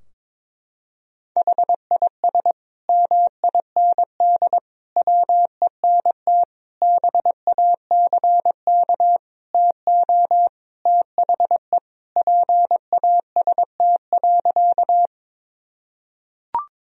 Morse code, heard as a keyed tone, about 22 wpm, 700 Hz. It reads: HIS MIND WENT BACK TO THE PAST.
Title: Sentences from Top 1000 Words - Review Part 3 - 22wpm